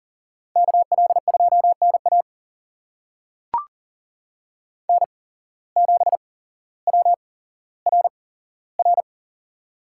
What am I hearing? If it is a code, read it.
Morse code, heard as a keyed tone, about 40 wpm, spaced as for 17 wpm, 700 Hz. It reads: KL2NA N7WRR
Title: USA Call Signs - 5 Characters - 8x Character Spacing - 40wpm